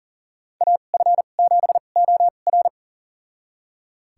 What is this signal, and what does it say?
Morse code, heard as a keyed tone, about 40 wpm, spaced as for 29 wpm, 700 Hz. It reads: AF7OR
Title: USA Call Signs - 5 Characters - 2x Character Spacing - 40wpm